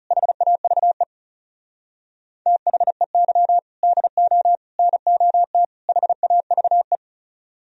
Morse code, heard as a keyed tone, about 39 wpm, 700 Hz. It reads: HAVE THEY DO NOT HAVE